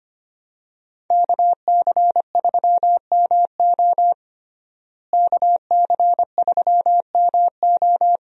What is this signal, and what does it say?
Morse code, heard as a keyed tone, about 25 wpm, 700 Hz. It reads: KC3MO KC3MO
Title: USA Call Signs - 5 Characters - Repeated 3x - 25wpm